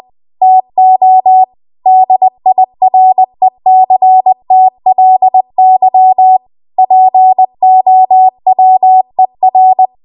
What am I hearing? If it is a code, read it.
Morse code, heard as a keyed tone, about 20 wpm, 750 Hz. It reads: TO DIRECTLY POWER
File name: Morse / 20WPM